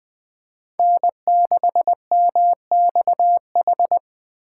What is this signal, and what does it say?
Morse code, heard as a keyed tone, about 24 wpm, 700 Hz. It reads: N6MXH
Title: Speed Racing - USA 1x3 Call Signs 20wpm